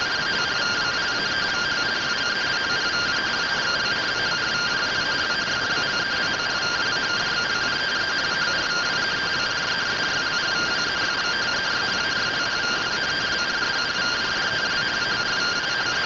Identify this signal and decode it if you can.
Two-tone teleprinter signal, about 170 Hz shift.